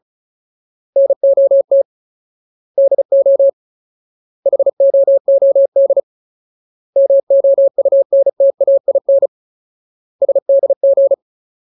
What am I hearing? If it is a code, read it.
Morse code, heard as a keyed tone, about 34 wpm, 550 Hz. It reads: NOT DO HOOD MOUNTAIN SDG
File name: Ultimate Word List / 35WPM Ultimate